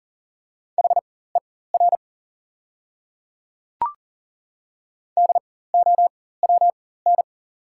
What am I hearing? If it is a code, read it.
Morse code, heard as a keyed tone, about 40 wpm, spaced as for 19 wpm, 690 Hz. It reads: HER DOWN
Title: 4x Character Spacing - Top 200 Words - 40wpm